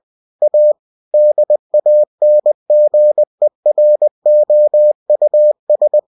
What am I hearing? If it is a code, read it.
Morse code, heard as a keyed tone, about 20 wpm, 600 Hz. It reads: A DANGEROUS
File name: cw-books / winnie-the-pooh / winnie-20wpm-page-0081